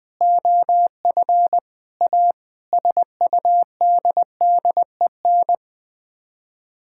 Morse code, heard as a keyed tone, about 21 wpm, 700 Hz. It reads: OF A SUDDEN
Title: Sets of 6 Words - 20wpm